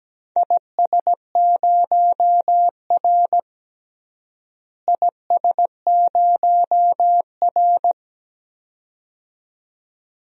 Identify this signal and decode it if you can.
Morse code, heard as a keyed tone, about 17 wpm, 700 Hz. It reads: IS0R IS0R
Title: DXCC-100 Call Signs - 4 Characters - Repeated 2x - 17wpm